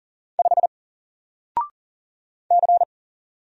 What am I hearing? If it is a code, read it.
Morse code, heard as a keyed tone, about 40 wpm, 700 Hz. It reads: 5 C
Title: Single Letter-Number Alt 40wpm